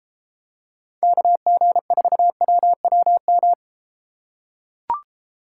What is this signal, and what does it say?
Morse code, heard as a keyed tone, about 33 wpm, 700 Hz. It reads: KG4WWM